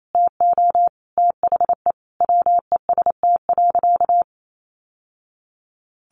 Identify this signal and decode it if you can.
Morse code, heard as a keyed tone, about 28 wpm, 700 Hz. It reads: TO THE WEST.
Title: Sentences from Top 1200 Words 28wpm